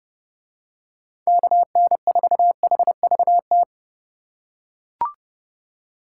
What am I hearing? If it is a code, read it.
Morse code, heard as a keyed tone, about 31 wpm, 700 Hz. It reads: KN4HVT